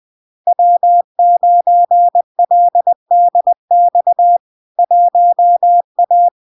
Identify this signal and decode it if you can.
Morse code, heard as a keyed tone, about 20 wpm, 700 Hz. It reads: W9LDX 1A
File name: LARK-FD-2022-Log-in-CW-20wpm-700-Hz